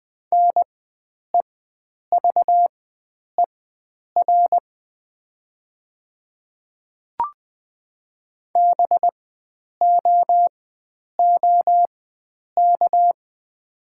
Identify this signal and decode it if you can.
Morse code, heard as a keyed tone, about 20 wpm, spaced as for 9 wpm, 700 Hz. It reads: NEVER BOOK
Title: 4x Character Spacing - Top 500 Words - 20wpm